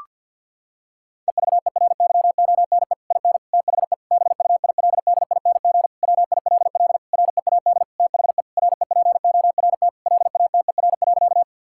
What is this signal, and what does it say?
Morse code, heard as a keyed tone, about 52 wpm, 700 Hz. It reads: EVERYONE IN THE BUILDING WILL READ THE REPORT LATER.
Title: Sentences from Top 1000 Words - Review Part 6 - 50wpm